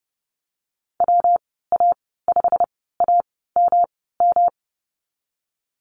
Morse code, heard as a keyed tone, about 30 wpm, spaced as for 17 wpm, 700 Hz. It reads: WA5AMM
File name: Lesson 19g - Letter w - Call Signs 30wpm